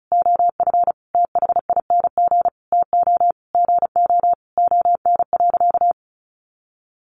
Morse code, heard as a keyed tone, about 34 wpm, 700 Hz. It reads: OF THING TO GO ON.
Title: Sentences from Top 1300 Words 35wpm